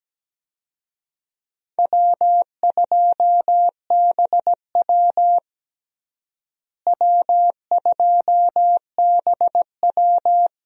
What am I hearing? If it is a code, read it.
Morse code, heard as a keyed tone, about 17 wpm, 700 Hz. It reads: W2BW W2BW
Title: USA Call Signs - 4 Characters - Repeated 3x - 17wpm